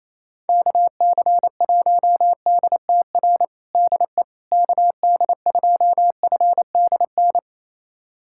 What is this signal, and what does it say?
Morse code, heard as a keyed tone, about 28 wpm, 700 Hz. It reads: KC1DTR DE KD2FDN
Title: USA Callsign DE Callsign 28wpm